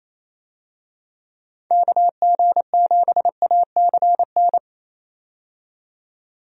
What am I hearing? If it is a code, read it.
Morse code, heard as a keyed tone, about 28 wpm, 700 Hz. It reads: KG7ACN
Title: USA Call Signs - 6 Characters - 28wpm